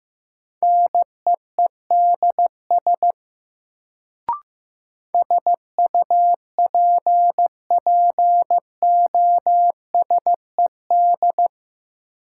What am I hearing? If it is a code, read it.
Morse code, heard as a keyed tone, about 15 wpm, 700 Hz. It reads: NEEDS SUPPOSED
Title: Sets of 3 Words - New Word Drill - 15wpm